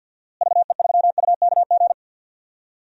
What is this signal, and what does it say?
Morse code, heard as a keyed tone, about 52 wpm, 700 Hz. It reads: VE3UKG
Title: DXCC-100 Call Signs - 6 Characters - Repeated 3x - 50wpm